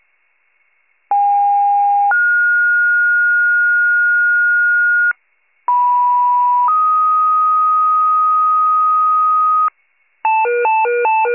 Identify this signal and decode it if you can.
Two-tone page, heard as a one-second tone, then a three-second tone, about 810 then 1430 Hz, about 980 then 1250 Hz, with noise. It is heard from 1.1 to 9.7 s.